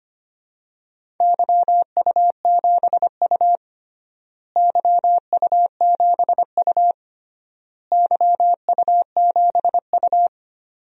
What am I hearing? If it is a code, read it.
Morse code, heard as a keyed tone, about 25 wpm, 700 Hz. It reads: YU7U YU7U YU7U
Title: DXCC-100 Call Signs - 4 Characters - Repeated 3x - 25wpm